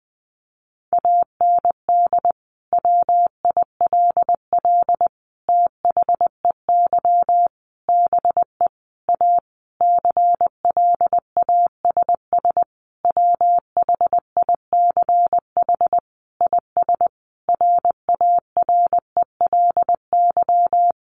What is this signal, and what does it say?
Morse code, heard as a keyed tone, about 19 wpm, 700 Hz. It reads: AND WILL THEY BE A CLASS WHICH IS RARELY
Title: Sentences from Top 1400 Words 20wpm